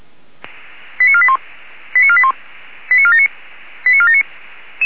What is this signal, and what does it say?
Five-tone selective call (ZVEI), heard as quick runs of five tones, tones of about 70 ms. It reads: 89481 89481 89479 89479